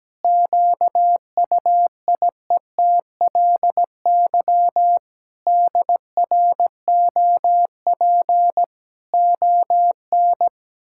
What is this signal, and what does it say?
Morse code, heard as a keyed tone, about 17 wpm, 700 Hz. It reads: QUIETLY DROP ON
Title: Pangram Challenge v2 - 17wpm